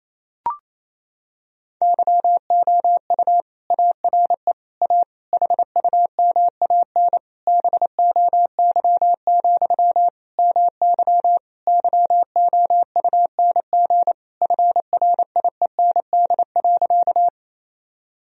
Morse code, heard as a keyed tone, about 28 wpm, 700 Hz. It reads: YOU ARE A HUMAN BOY, MY YOUNG FRIEND.
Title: Sentences from Top 1000 Words - Review Part 3 - 28wpm